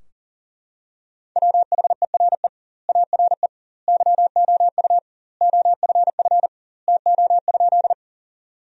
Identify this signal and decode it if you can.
Morse code, heard as a keyed tone, about 40 wpm, 700 Hz. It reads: WHERE ARE YOU OFF TO?
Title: 2x Word-Spacing - Sentences from Top 200 Words - 40wpm